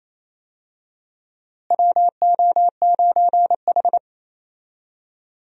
Morse code, heard as a keyed tone, about 27 wpm, 700 Hz. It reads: WO9H
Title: USA Call Signs - 4 Characters - Repeated 3x - 28wpm